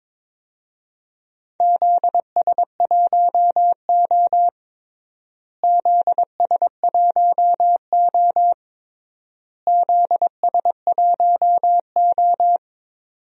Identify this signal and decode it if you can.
Morse code, heard as a keyed tone, about 22 wpm, 700 Hz. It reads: ZS1O ZS1O ZS1O
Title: DXCC-100 Call Signs - 4 Characters - Repeated 3x - 22wpm